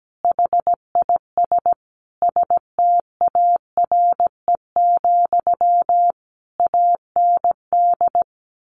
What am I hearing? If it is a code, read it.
Morse code, heard as a keyed tone, about 17 wpm, 700 Hz. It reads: HIS STARE, AND
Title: A Sound Of Thunder 17wpm